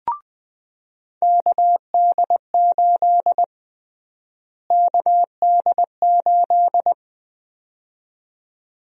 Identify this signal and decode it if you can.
Morse code, heard as a keyed tone, about 20 wpm, 700 Hz. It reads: KD8 KD8
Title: USA Call Sign Prefixes - Repeated 2x - 20wpm